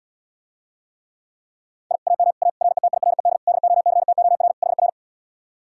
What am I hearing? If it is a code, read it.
Morse code, heard as a keyed tone, about 80 wpm, 700 Hz. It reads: I AM A RETIRED DOCTOR HR